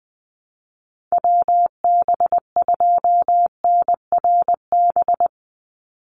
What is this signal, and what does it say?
Morse code, heard as a keyed tone, about 20 wpm, 700 Hz. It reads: WB2NRB
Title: USA Call Signs 20wpm